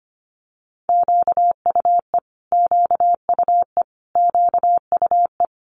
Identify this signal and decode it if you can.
Morse code, heard as a keyed tone, about 25 wpm, 700 Hz. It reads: QUE QUE QUE
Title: Multi-letter Phonemes Repeated 3x 25wpm